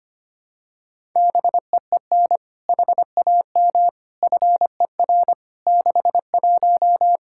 Morse code, heard as a keyed tone, about 25 wpm, 700 Hz. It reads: BEEN HAM FER 61
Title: Practice QSO - Exchange 3 - More Info - 25wpm